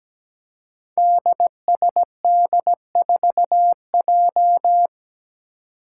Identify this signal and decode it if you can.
Morse code, heard as a keyed tone, about 17 wpm, 700 Hz. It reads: DSD4J
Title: Five Challenging Letter-Number Combinations 17wpm